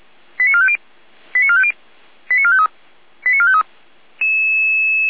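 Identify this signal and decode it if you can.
Five-tone selective call (ZVEI), heard as quick runs of five tones, tones of about 70 ms. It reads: 89450 89450 89453 89453
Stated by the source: Five-tone ZVEI selective call received by SDR, archived by Feuerwehr Orlamünde.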